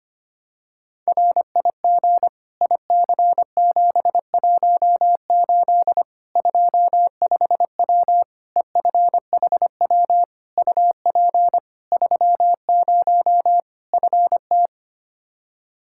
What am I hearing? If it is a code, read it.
Morse code, heard as a keyed tone, about 25 wpm, 700 Hz. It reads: RIG IC718 25W EFHW UP 30 FT
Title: Practice QSO - Exchange 2 - Info - 25wpm